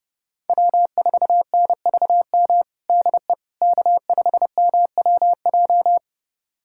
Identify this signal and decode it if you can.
Morse code, heard as a keyed tone, about 32 wpm, 700 Hz. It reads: W4NVM DE K5MWJ
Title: USA Callsign DE Callsign - 5 Characters - 30wpm